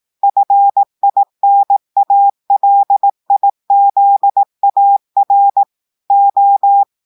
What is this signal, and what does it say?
Morse code, heard as a keyed tone, about 18 wpm, 800 Hz. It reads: FINALIZAR O
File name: portugais texte en clair 2 vitesse 18 mots minutes